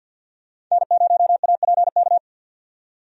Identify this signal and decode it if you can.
Morse code, heard as a keyed tone, about 46 wpm, 700 Hz. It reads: N0APK
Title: USA Call Signs - 5 Characters - 50wpm